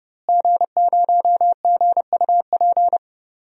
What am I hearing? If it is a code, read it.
Morse code, heard as a keyed tone, about 30 wpm, 700 Hz. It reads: G0GUP
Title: DXCC-100 Call Signs - 5 Characters - Repeated 3x - 30wpm